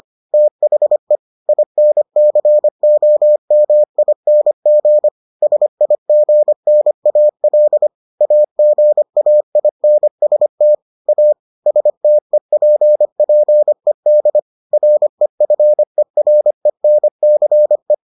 Morse code, heard as a keyed tone, about 25 wpm, 600 Hz. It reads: THE INCOMING SIGNAL AGAINST A STEPPED REFERENCE